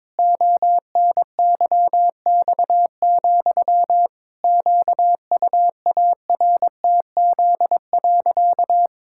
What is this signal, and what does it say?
Morse code, heard as a keyed tone, about 22 wpm, 700 Hz. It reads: ONYX, QUARTZ.
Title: Pangram Challenge v2 - 22wpm